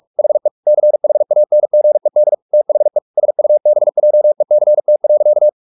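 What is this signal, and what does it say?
Morse code, heard as a keyed tone, about 44 wpm, 600 Hz. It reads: HE CHANGED THE SUBJECT.
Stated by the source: Morse practice at 45 wpm, the novel Never by Ken Follett in English, from CW med Gnister.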